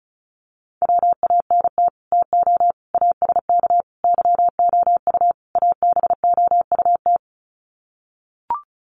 Morse code, heard as a keyed tone, about 34 wpm, 700 Hz. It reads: WANT TO ASK YOU ABOUT
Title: Sets of 5 Words Encore2 35wpm